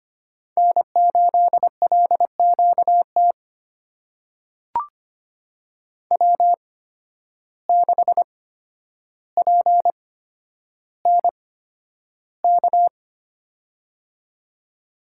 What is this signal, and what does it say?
Morse code, heard as a keyed tone, about 25 wpm, spaced as for 11 wpm, 700 Hz. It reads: N8LQT W6PNK
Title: USA Call Signs - 5 Characters - 8x Character Spacing - 25wpm